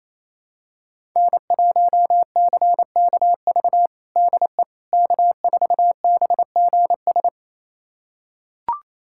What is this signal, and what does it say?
Morse code, heard as a keyed tone, about 28 wpm, 700 Hz. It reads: N1CKV DE K4BGS
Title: USA Callsign DE Callsign - 5 Characters - 28wpm